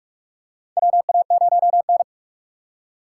Morse code, heard as a keyed tone, about 46 wpm, 700 Hz. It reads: WA0N
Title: USA Call Signs - 4 Characters - 2x Character Spacing - 45wpm